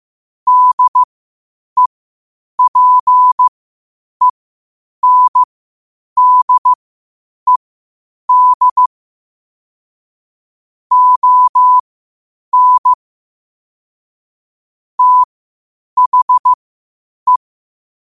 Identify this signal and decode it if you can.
Morse code, heard as a keyed tone, about 15 wpm, spaced as for 8 wpm, 1000 Hz. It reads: DEPENDED ON THE
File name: qotdfiles / 8wpm